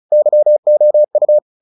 Morse code, heard as a keyed tone, about 34 wpm, 600 Hz. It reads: YOU
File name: the old man and the sea / the old man and the sea 1 35wpm